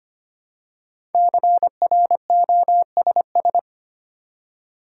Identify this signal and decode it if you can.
Morse code, heard as a keyed tone, about 25 wpm, 700 Hz. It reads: CROSS
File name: 5 Letter Words - First 3 Letters Given - 25wpm